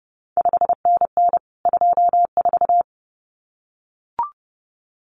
Morse code, heard as a keyed tone, about 30 wpm, 700 Hz. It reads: 5NN 24
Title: CQ WW DX Contest Exchange Encore - 30wpm